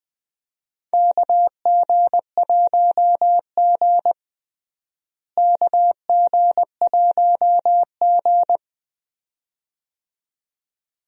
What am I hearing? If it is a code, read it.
Morse code, heard as a keyed tone, about 21 wpm, 700 Hz. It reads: KG1G KG1G
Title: USA Call Signs - 4 Characters - Repeated 2x - 20wpm